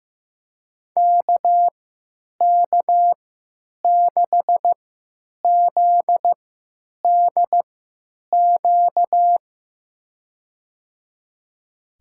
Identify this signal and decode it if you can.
Morse code, heard as a keyed tone, about 15 wpm, spaced as for 9 wpm, 700 Hz. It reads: KK6ZDQ
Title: USA Call Signs - 6 Characters - 3x Character Spacing - 15wpm